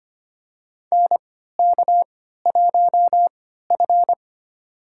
Morse code, heard as a keyed tone, about 25 wpm, spaced as for 14 wpm, 700 Hz. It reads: NK1F
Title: USA Call Signs - 4 Characters - 3x Character Spacing - 25wpm